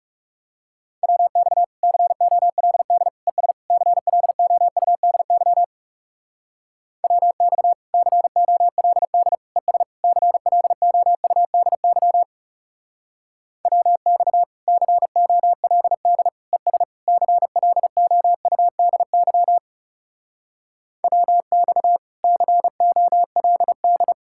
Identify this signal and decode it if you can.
Morse code, heard as a keyed tone, about 37 wpm, 700 Hz. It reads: WX COLD ES CLOUDY WX COLD ES CLOUDY WX COLD ES CLOUDY WX COLD
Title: Speed-Racing Common QSO Phrases - 30wpm